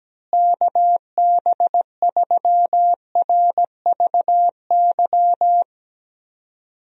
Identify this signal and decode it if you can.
Morse code, heard as a keyed tone, about 17 wpm, 700 Hz. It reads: KB3RVY
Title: USA Call Signs - 6 Characters - Repeated 3x - 17wpm